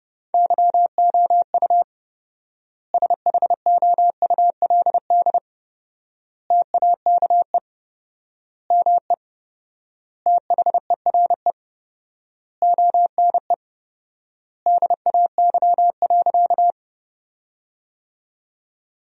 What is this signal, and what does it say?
Morse code, heard as a keyed tone, about 31 wpm, 700 Hz. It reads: YOU SHOULD TAKE ME THERE ONE DAY.
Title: 4x Word-Spacing - Sentences from Top 300 Words - 30wpm